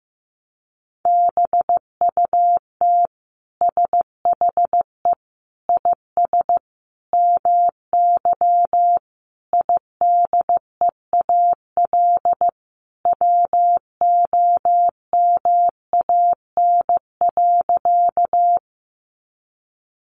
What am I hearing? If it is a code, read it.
Morse code, heard as a keyed tone, about 15 wpm, 700 Hz. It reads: BUT SHE IS MY IDEAL WOMAN.
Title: Sentences from Top 1400 Words 15wpm